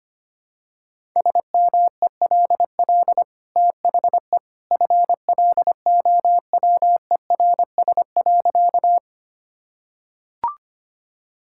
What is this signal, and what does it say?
Morse code, heard as a keyed tone, about 25 wpm, 690 Hz. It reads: SMELL THE FLOWERS.
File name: Easy Sentences with Hints - 25wpm